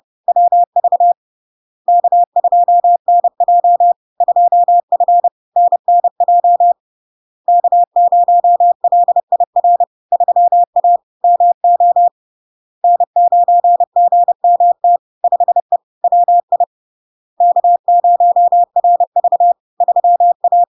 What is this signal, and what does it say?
Morse code, heard as a keyed tone, about 30 wpm, 700 Hz. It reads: WV K2NJ 2F NNJ K0LIR 3A MO N9GMT 5E WI K0RV 3A
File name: LARK-FD-2022-Log-in-CW-30wpm-700-Hz